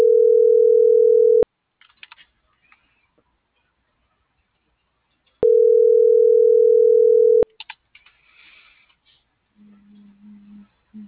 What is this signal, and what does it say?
Ringback tone (North America), 440 + 480 Hz.